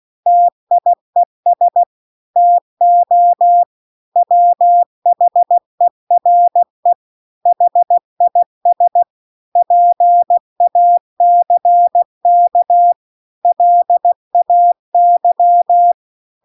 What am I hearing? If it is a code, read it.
Morse code, heard as a keyed tone, about 16 wpm, 700 Hz. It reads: TIES TO WHERE HIS PACK LAY